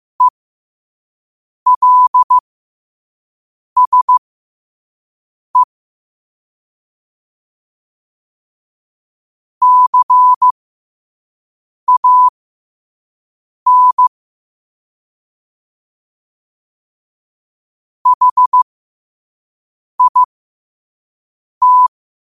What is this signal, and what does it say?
Morse code, heard as a keyed tone, about 15 wpm, spaced as for 5 wpm, 1000 Hz. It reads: ELSE CAN HIT